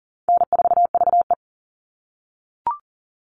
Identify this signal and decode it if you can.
Morse code, heard as a keyed tone, about 40 wpm, 700 Hz. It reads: N4VE
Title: All Cut Numbers - 4-Digit 40wpm